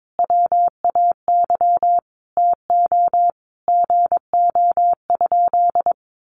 Morse code, heard as a keyed tone, about 22 wpm, 700 Hz. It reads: WAY TO GO?
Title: Sentences from Top 1300 Words 22wpm